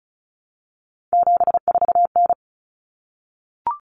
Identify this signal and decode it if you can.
Morse code, heard as a keyed tone, about 34 wpm, 700 Hz. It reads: 74N
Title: Common Cut Numbers - 3-Digit 35wpm